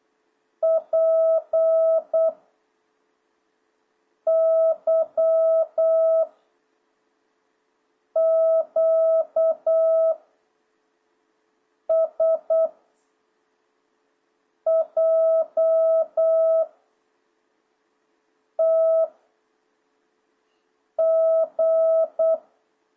Morse code, heard as a keyed tone, about 8 wpm, 650 Hz. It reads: P Y Q S J T G